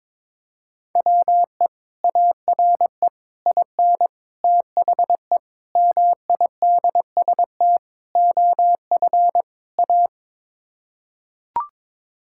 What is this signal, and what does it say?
Morse code, heard as a keyed tone, about 22 wpm, 700 Hz. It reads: WE ARE IN THE MIDST OF A
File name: practice / 22wpm / Sets of 7 Words - 22wpm